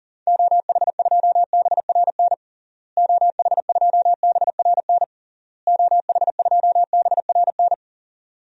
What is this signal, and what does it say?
Morse code, heard as a keyed tone, about 40 wpm, 700 Hz. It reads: OH2BRN OH2BRN OH2BRN
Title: DXCC-100 Call Signs - 6 Characters - Repeated 3x - 40wpm